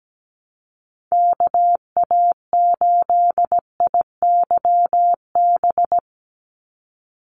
Morse code, heard as a keyed tone, about 17 wpm, 700 Hz. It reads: KA8IYB
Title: USA Call Signs 17wpm